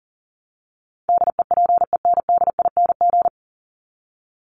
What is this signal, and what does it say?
Morse code, heard as a keyed tone, about 40 wpm, 700 Hz. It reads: DEPENDING
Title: ING Suffix - Top 100 Words with Base Words 40wpm